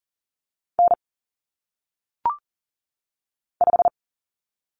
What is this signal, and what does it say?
Morse code, heard as a keyed tone, about 39 wpm, 700 Hz. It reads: N 5
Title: Common Cut Numbers - 1-Digit 40wpm